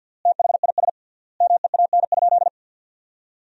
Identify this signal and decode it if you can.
Morse code, heard as a keyed tone, about 53 wpm, 700 Hz. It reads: THIS MEAN?
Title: 3x Word-Spacing - Sentences from Top 200 Words - 50wpm